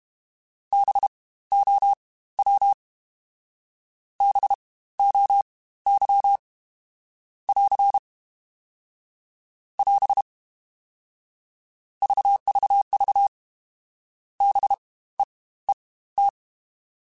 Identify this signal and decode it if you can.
Morse code, heard as a keyed tone, about 32 wpm, spaced as for 17 wpm, 800 Hz. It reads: BOW BOY <AR> <AS> VVV BEET
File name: Spaced Alphas / Spaced Alphas mp3s / Bs sp @30